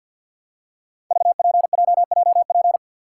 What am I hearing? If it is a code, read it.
Morse code, heard as a keyed tone, about 47 wpm, 700 Hz. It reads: VPJJP
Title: Code Groups - 5 Letters - 50wpm